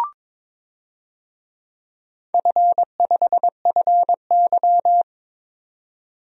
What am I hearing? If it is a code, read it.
Morse code, heard as a keyed tone, about 22 wpm, 700 Hz. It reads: F5FY
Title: DXCC-100 Call Signs - 4 Characters - 22wpm